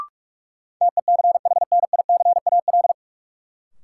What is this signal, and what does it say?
Morse code, heard as a keyed tone, about 46 wpm, 700 Hz. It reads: TECHNICAL
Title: Sentences from Top 1100 Words - New Word Drill Alt - 45wpm